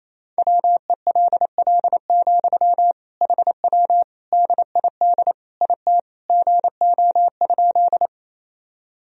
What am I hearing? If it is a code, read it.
Morse code, heard as a keyed tone, about 28 wpm, 700 Hz. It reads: WELL, HW DID IT GO?
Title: Sentences from Top 200 Words and Abbreviations 28wpm